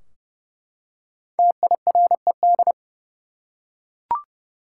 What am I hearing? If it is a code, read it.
Morse code, heard as a keyed tone, about 30 wpm, 700 Hz. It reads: TIRED
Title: Five Letter Words - 30wpm